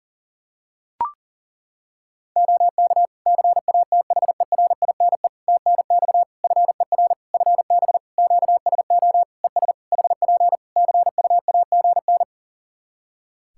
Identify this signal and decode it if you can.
Morse code, heard as a keyed tone, about 40 wpm, 700 Hz. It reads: OK CATHERINE TNX FER FB QSO ES HP CUAGN
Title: Practice QSO - Exchange 4 - Ending - 40wpm